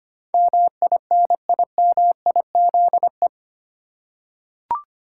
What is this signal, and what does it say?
Morse code, heard as a keyed tone, about 25 wpm, 700 Hz. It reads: MINIMIZE
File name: Eight Letter Words - 25wpm